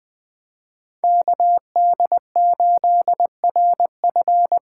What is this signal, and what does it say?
Morse code, heard as a keyed tone, about 20 wpm, 700 Hz. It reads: KD8RF